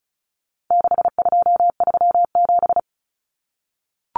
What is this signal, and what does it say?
Morse code, heard as a keyed tone, about 34 wpm, 700 Hz. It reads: 6237
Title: Numbers - 4-Digit 35wpm